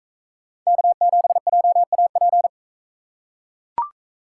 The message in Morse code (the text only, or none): K7JAP